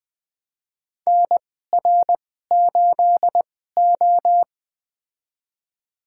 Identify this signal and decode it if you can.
Morse code, heard as a keyed tone, about 21 wpm, spaced as for 15 wpm, 700 Hz. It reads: NR8O
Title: USA Call Signs - 4 Characters - 2x Character Spacing - 20wpm